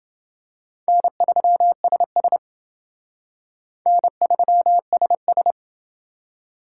Morse code, heard as a keyed tone, about 28 wpm, 700 Hz. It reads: N3SS N3SS